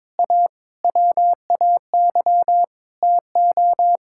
Morse code, heard as a keyed tone, about 22 wpm, 700 Hz. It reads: A WAY TO
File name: practice / 22wpm / Sets of 7 Words - 22wpm